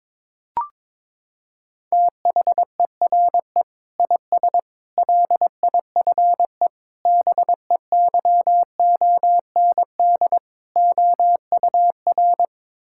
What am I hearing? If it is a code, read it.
Morse code, heard as a keyed tone, about 22 wpm, 700 Hz. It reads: THERE IS LIFE BEYOND OUR
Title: Sentences from Top 1000 Words - Review Part 6 - 22wpm